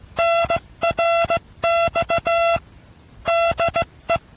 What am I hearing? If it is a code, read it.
Morse code, heard as a keyed tone, about 16 wpm, 680 Hz, with noise. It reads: NRX DE